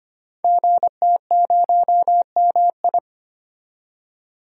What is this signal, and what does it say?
Morse code, heard as a keyed tone, about 25 wpm, 700 Hz. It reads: GT0MI
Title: Code Groups - 5 Letter:Numbers - 25wpm